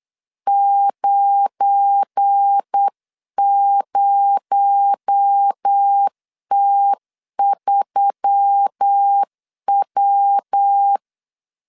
Morse code, heard as a keyed tone, about 8 wpm, 800 Hz. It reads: 90T3W
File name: CW / DET / 1 9 1500C S A 8wpm 5637